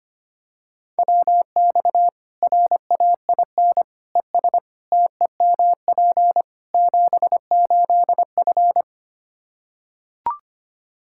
Morse code, heard as a keyed tone, about 25 wpm, 700 Hz. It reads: WX RAIN ES TEMP 78F
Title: K1USN SST QSO Practice 25wpm